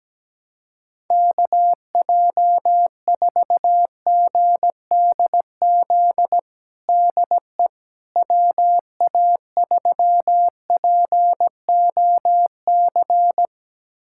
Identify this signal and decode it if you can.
Morse code, heard as a keyed tone, about 17 wpm, 700 Hz. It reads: KJ4GDZ DE WA3POC
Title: USA Callsign DE Callsign 17wpm